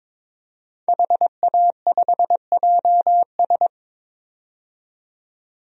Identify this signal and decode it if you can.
Morse code, heard as a keyed tone, about 22 wpm, 700 Hz. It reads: HA5JS
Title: DXCC-100 Call Signs - 5 Characters - 22wpm